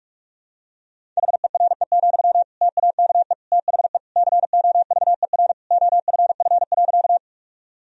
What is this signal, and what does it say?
Morse code, heard as a keyed tone, about 46 wpm, 700 Hz. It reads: HERE, TAKE THE COVER OFF.
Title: Sentences from Top 1000 Words - Review Part 1 - 45wpm